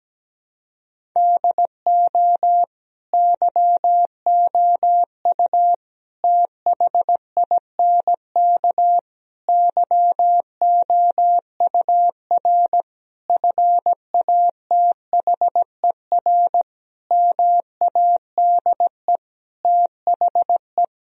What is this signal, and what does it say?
Morse code, heard as a keyed tone, about 17 wpm, 700 Hz. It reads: DO YOU THINK YOUR FATHER MADE THE
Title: Sentences from Top 1000 Words - Review Part 1 - 17wpm